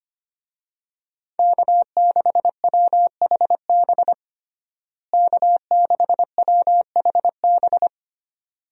Morse code, heard as a keyed tone, about 25 wpm, 700 Hz. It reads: K6WHB K6WHB
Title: USA Call Signs - 5 Characters - Repeated 2x - 25wpm